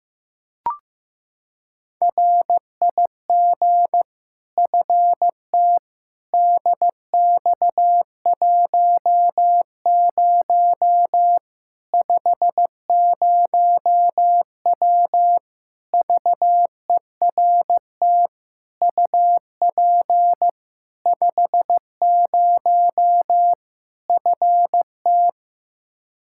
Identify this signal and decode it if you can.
Morse code, heard as a keyed tone, about 15 wpm, 700 Hz. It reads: RIG FTDX10 50W VERT UP 50 FT
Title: Practice QSO - Exchange 2 - Info - 15wpm